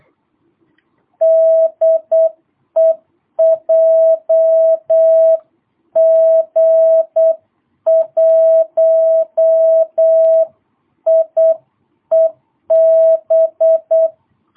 Morse code, heard as a keyed tone, about 8 wpm, 650 Hz. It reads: DE JG1IEB
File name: jh8cbh / 2022cw